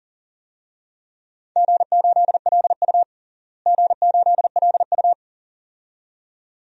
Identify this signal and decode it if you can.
Morse code, heard as a keyed tone, about 40 wpm, 700 Hz. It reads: G8LU G8LU
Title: DXCC-100 Call Signs - 4 Characters - Repeated 2x - 40wpm